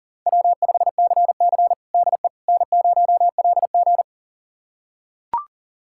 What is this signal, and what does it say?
Morse code, heard as a keyed tone, about 40 wpm, 700 Hz. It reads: W5CC DE N0LG
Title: USA Callsign DE Callsign - 4 Characters - 40wpm